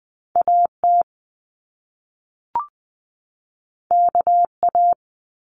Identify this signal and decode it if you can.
Morse code, heard as a keyed tone, about 20 wpm, 700 Hz. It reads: AT KA
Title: Top Two Letter Combinations in Words 20wpm